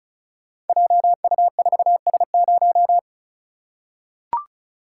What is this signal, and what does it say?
Morse code, heard as a keyed tone, about 36 wpm, 690 Hz. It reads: JU4S0
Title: Code Groups - 5 Letter:Numbers - 35wpm